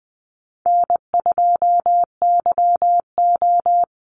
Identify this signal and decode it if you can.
Morse code, heard as a keyed tone, about 20 wpm, 700 Hz. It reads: N2YO
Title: Easy Call Signs Repeated 2x - 20wpm